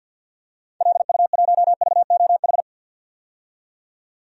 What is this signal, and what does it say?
Morse code, heard as a keyed tone, about 46 wpm, 700 Hz. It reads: LU1VOH
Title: DXCC-100 Call Signs - 6 Characters - 50wpm